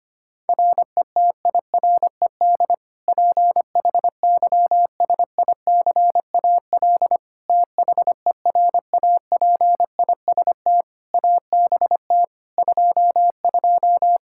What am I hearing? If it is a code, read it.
Morse code, heard as a keyed tone, about 25 wpm, 700 Hz. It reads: RETIRED PHYSICAL THERAPIST ABT 22
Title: Practice QSO - Exchange 3 - More Info - 25wpm